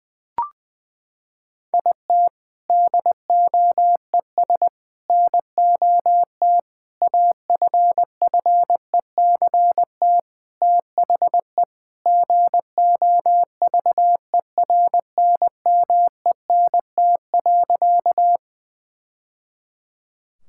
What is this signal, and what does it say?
Morse code, heard as a keyed tone, about 20 wpm, 700 Hz. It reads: IT DOES NOT AFFECT THE GOVERNMENT.
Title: Sentences from Top 1000 Words - Review Part 6 - 20wpm